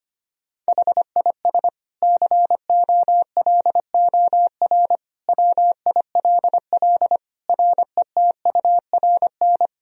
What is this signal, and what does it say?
Morse code, heard as a keyed tone, about 25 wpm, 700 Hz. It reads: HIS COLOR WILL RETURN
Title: Sentences from Top 1000 Words - Review Part 2 - 25wpm